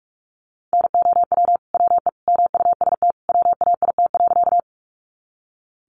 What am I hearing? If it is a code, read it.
Morse code, heard as a keyed tone, about 44 wpm, 700 Hz. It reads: NOW WE MUST WAIT.
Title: Sentences from Top 500 Words 45wpm